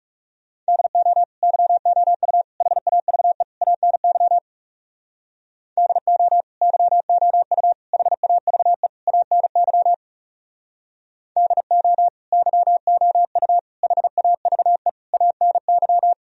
Morse code, heard as a keyed tone, about 36 wpm, 700 Hz. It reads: DO YOU HAVE ANY DO YOU HAVE ANY DO YOU HAVE ANY
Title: Speed-Racing - Sets of 4 Words 30wpm